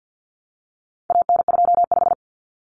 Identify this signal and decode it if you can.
Morse code, heard as a keyed tone, about 50 wpm, 700 Hz. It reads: AN25
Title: Common Cut Numbers - 4-Digit 50wpm